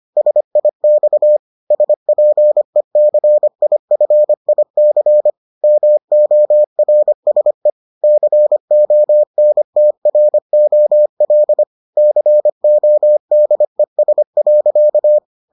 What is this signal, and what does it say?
Morse code, heard as a keyed tone, about 25 wpm, 600 Hz. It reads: SIX SPECIFIC MORSE CONTROL CODES.